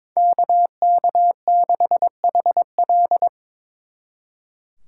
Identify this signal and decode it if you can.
Morse code, heard as a keyed tone, about 26 wpm, 700 Hz. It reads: KK6HL
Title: Speed Racing - USA 2x2 Call Signs 22wpm